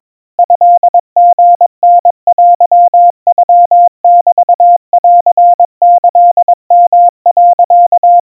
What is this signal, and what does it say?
Morse code (Japanese Wabun code), heard as a keyed tone, about 22 wpm, 700 Hz. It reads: トリタテノメンキヨ、
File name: 80 cpm 700 Hz Crispy